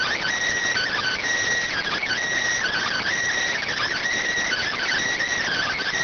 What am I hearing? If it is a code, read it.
Emergency Alert System (SAME) tones (header bursts), heard as harsh buzzing bursts. It is heard from 0.7 to 5.0 s.